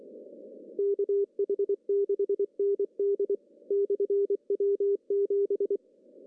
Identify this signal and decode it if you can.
Morse code, heard as a keyed tone, about 23 wpm, 400 Hz. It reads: KH6ND/W7
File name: CW / cqww-cw-k5tr